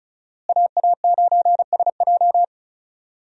Morse code, heard as a keyed tone, about 36 wpm, 700 Hz. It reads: AA9SJ
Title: USA Call Signs - 5 Characters - Repeated 2x - 35wpm